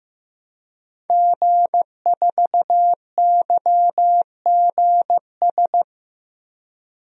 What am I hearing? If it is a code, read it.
Morse code, heard as a keyed tone, about 15 wpm, 700 Hz. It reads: G4YGS
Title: DXCC-100 Call Signs - 5 Characters - 15wpm